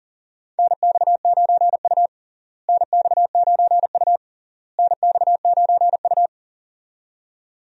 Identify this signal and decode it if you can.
Morse code, heard as a keyed tone, about 40 wpm, 700 Hz. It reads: NX9U NX9U NX9U
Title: USA Call Signs - 4 Characters - Repeated 3x - 40wpm